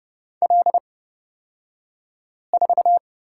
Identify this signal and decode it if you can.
Morse code, heard as a keyed tone, about 30 wpm, spaced as for 5 wpm, 700 Hz. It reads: L4
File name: Lesson 38d - Letter q - Two Characters 30wpm